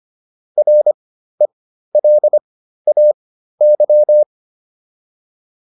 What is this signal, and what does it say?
Morse code, heard as a keyed tone, about 25 wpm, spaced as for 13 wpm, 600 Hz. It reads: RELAY